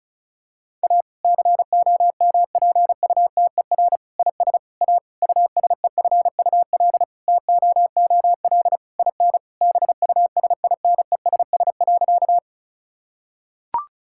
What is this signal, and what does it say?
Morse code, heard as a keyed tone, about 34 wpm, 700 Hz. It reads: A COMPUTER IS A USEFUL TOOL IN BUSINESS.
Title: Sentences from Top 1000 Words - Review Part 6 - 35wpm